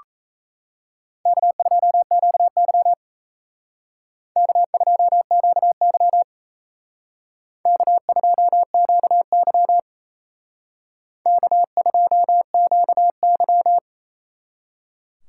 Morse code, heard as a keyed tone, about 35 wpm, 700 Hz. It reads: K2QY K2QY K2QY K2QY